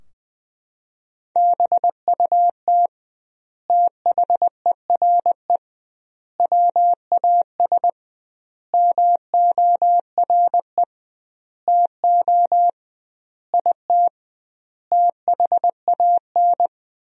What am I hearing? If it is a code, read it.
Morse code, heard as a keyed tone, about 21 wpm, 700 Hz. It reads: BUT THERE WAS MORE TO IT THAN
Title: 2x Word-Spacing - Sentences from Top 100 Words - 20wpm